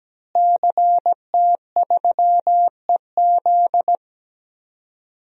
Morse code, heard as a keyed tone, about 17 wpm, 700 Hz. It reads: CT3EZ
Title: DXCC-100 Call Signs - 5 Characters - Repeated 3x - 17wpm